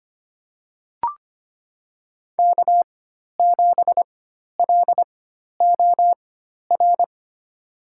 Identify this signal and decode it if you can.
Morse code, heard as a keyed tone, about 25 wpm, spaced as for 12 wpm, 700 Hz. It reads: K7LOR
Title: USA Call Signs - 4x Character Spacing - 25wpm